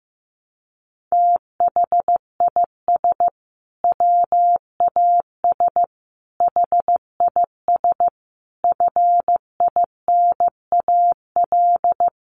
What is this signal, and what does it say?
Morse code, heard as a keyed tone, about 15 wpm, 700 Hz. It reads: THIS WAS HIS FINAL
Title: Sentences from Top 1300 Words 15wpm